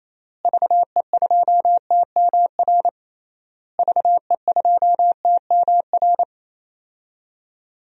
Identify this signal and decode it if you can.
Morse code, heard as a keyed tone, about 28 wpm, 700 Hz. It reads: VE2TMR VE2TMR
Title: DXCC-100 Call Signs - 6 Characters - Repeated 3x - 28wpm